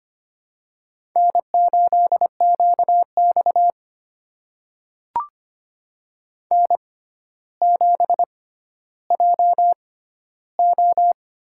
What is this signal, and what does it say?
Morse code, heard as a keyed tone, about 25 wpm, spaced as for 13 wpm, 700 Hz. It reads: N8QX N7JO
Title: USA Call Signs - 4 Characters - 6x Character Spacing - 25wpm